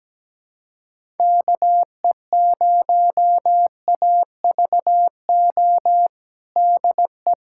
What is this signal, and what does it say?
Morse code, heard as a keyed tone, about 17 wpm, 700 Hz. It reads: KE0AVO DE
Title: USA Callsign DE Callsign 17wpm